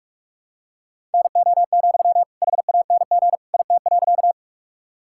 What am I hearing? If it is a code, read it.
Morse code, heard as a keyed tone, about 41 wpm, 700 Hz. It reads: NO, HANG IT.
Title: Sentences from Top 1600 Words 45wpm